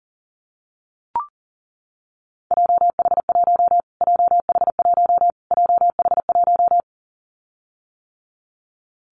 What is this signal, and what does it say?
Morse code, heard as a keyed tone, about 40 wpm, 700 Hz. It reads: JH1 JH1 JH1
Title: Callsign Prefixes for Top 25 DXCC - Repeated 3x - 40wpm